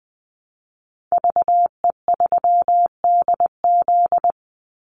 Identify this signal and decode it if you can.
Morse code, heard as a keyed tone, about 20 wpm, 700 Hz. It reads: VE3DZ
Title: Call Signs - 20wpm